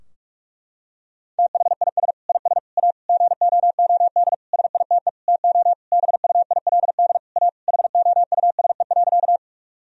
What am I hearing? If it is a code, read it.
Morse code, heard as a keyed tone, about 46 wpm, 700 Hz. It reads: THIS IS A GOOD SITE TO BUILD A HOUSE.
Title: Sentences from Top 1000 Words - Review Part 4 - 45wpm